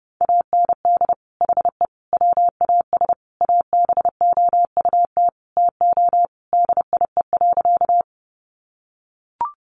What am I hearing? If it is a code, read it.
Morse code, heard as a keyed tone, about 30 wpm, 700 Hz. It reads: AND HE WAS ABOUT TO DIE.
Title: Sentences from Top 1300 Words 30wpm